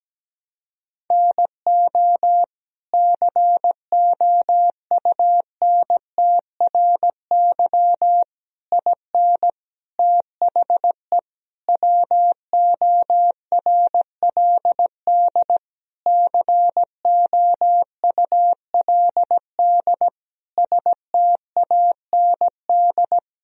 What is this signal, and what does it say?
Morse code, heard as a keyed tone, about 17 wpm, 700 Hz. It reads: NO COUNTRY IN THE WORLD COULD STAND
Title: Sentences from Top 1000 Words - Review Part 3 - 17wpm